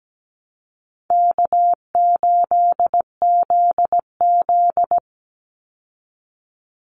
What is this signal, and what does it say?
Morse code, heard as a keyed tone, about 17 wpm, 700 Hz. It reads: K8ZZ
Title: Call Signs Repeated 3x - 17wpm